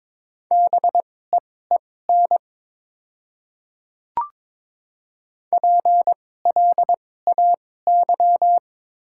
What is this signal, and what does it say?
Morse code, heard as a keyed tone, about 22 wpm, spaced as for 16 wpm, 700 Hz. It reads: BEEN PLAY
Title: 2x Character Spacing - Top 200 Words - 22wpm